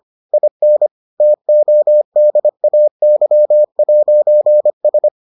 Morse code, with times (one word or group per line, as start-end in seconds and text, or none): 0.3-0.9 IN
1.2-5.1 TODAY'S